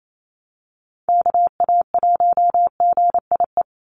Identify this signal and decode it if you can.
Morse code, heard as a keyed tone, about 28 wpm, 700 Hz. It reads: KA1GIE
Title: USA Call Signs 28wpm